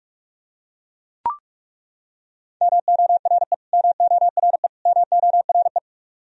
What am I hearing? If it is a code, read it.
Morse code, heard as a keyed tone, about 41 wpm, 700 Hz. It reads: MORE MORE MORE
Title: Top 100 Words Repeated 45wpm